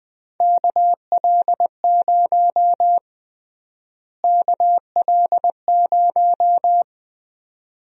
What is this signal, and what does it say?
Morse code, heard as a keyed tone, about 21 wpm, 700 Hz. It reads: KL0 KL0
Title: USA Call Sign Prefixes - Repeated 2x - 20wpm